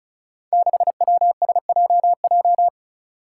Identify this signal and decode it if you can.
Morse code, heard as a keyed tone, about 34 wpm, 700 Hz. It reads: 6WSJJ
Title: Five Challenging Letter-Number Combinations 35wpm